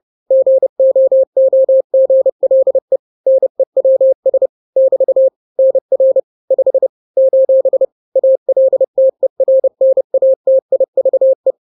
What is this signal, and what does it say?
Morse code, heard as a keyed tone, about 30 wpm, 530 Hz. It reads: GOOGLE NEWS <BT> NR 5 : ALTERNATIVE
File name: daily / 29wpm